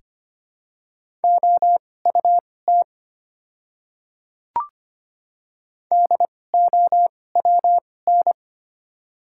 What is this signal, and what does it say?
Morse code, heard as a keyed tone, about 25 wpm, spaced as for 18 wpm, 700 Hz. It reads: OUT DOWN
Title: 2x Character Spacing - Top 100 Words - 25wpm